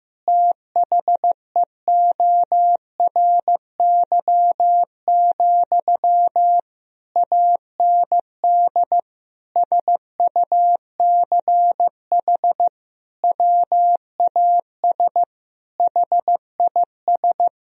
Morse code, heard as a keyed tone, about 15 wpm, 700 Hz. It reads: THEORY, AND SUCH WAS HIS
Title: Sentences from Top 1000 Words - Review Part 4 - 15wpm